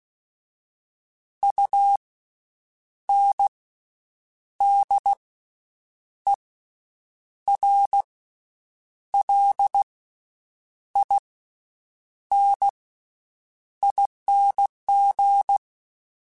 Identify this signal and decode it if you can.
Morse code, heard as a keyed tone, about 16 wpm, spaced as for 7 wpm, 800 Hz. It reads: UNDERLINING
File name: Special INGs-mp3s / Special INGs-mp3s / Slow ing @15